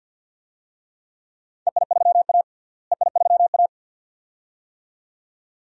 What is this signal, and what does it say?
Morse code, heard as a keyed tone, about 51 wpm, 700 Hz. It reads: EI3A EI3A
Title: DXCC-100 Call Signs - 4 Characters - Repeated 2x - 50wpm